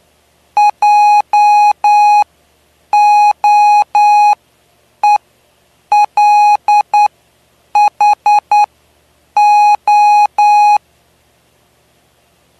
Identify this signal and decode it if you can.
Morse code, heard as a keyed tone, about 9 wpm, spaced as for 7 wpm, 830 Hz. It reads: JOELHO
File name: cw / py2lcd / 09-PY2LCD-Li��o 9